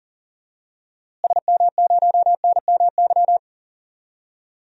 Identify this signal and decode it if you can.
Morse code, heard as a keyed tone, about 40 wpm, 700 Hz. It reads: SM0NMY
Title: DXCC-100 Call Signs - 6 Characters - Repeated 2x - 40wpm